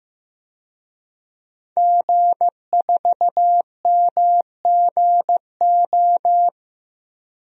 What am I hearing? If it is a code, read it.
Morse code, heard as a keyed tone, about 16 wpm, 700 Hz. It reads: G4MGO
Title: DXCC-100 Call Signs - 5 Characters - Repeated 2x - 15wpm